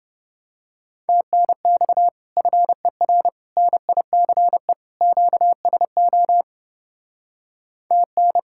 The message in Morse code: TNX FER NICE QSO TN

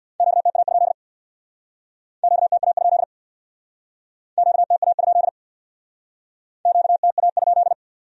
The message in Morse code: QTA? QTA? QTA? QTA?

